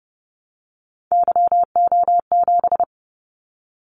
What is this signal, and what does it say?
Morse code, heard as a keyed tone, about 30 wpm, 700 Hz. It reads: YO7
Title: Callsign Prefixes for Top 25 DXCC - 30wpm